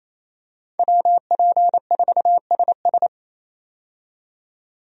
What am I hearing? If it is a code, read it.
Morse code, heard as a keyed tone, about 28 wpm, 700 Hz. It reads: WP4SS
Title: USA Call Signs - 5 Characters - 28wpm